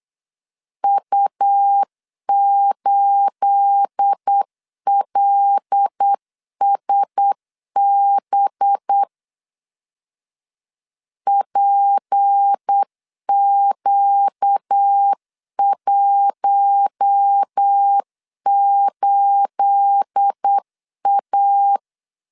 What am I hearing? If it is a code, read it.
Morse code, heard as a keyed tone, about 8 wpm, 800 Hz. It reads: U8LSB PQ18A
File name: LEZIONI / 9 28 8wpm 3512